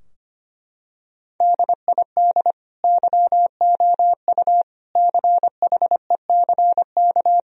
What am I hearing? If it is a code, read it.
Morse code, heard as a keyed tone, about 25 wpm, 700 Hz. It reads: DID YOU CHECK
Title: Sentences from Top 1000 Words - Review Part 4 - 25wpm